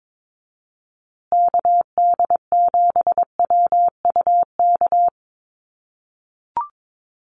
Morse code, heard as a keyed tone, about 22 wpm, 700 Hz. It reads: KD7WUK
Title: USA Call Signs Repeated 3x 22wpm